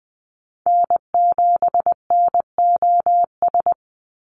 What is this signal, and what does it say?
Morse code, heard as a keyed tone, about 20 wpm, 700 Hz. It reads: N7NOS
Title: USA Call Signs 20wpm